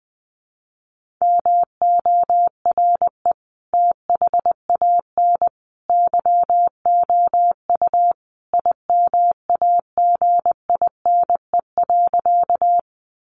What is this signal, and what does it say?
Morse code, heard as a keyed tone, about 20 wpm, 700 Hz. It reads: MORE THAN YOU IMAGINE.
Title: Sentences from Top 1400 Words 20wpm